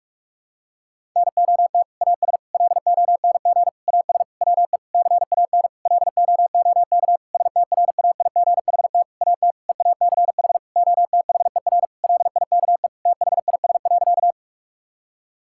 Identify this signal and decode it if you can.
Morse code, heard as a keyed tone, about 46 wpm, 700 Hz. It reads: NOT AS LONG AS WE CAN LOOK STRAIGHT AT EACH OTHER LIKE THIS.
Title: Sentences from Top 1000 Words - Review Part 6 - 45wpm